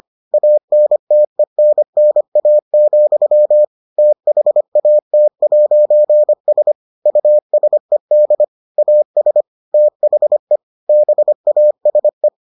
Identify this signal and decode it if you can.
Morse code, heard as a keyed tone, about 25 wpm, 600 Hz. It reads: ANTENNA, THAT'S USED AS THE BASE